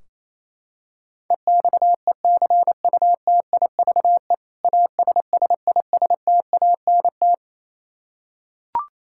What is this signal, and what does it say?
Morse code, heard as a keyed tone, about 28 wpm, 700 Hz. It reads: EXECUTIVE ASSISTANT
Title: Practice QSO - Exchange 3 - More Info - 28wpm